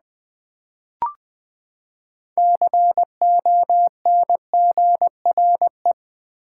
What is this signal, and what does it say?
Morse code, heard as a keyed tone, about 20 wpm, 700 Hz. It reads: CONGRE
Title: Build A Word - 6 Letters At A Time - 12-Letter Words - 20wpm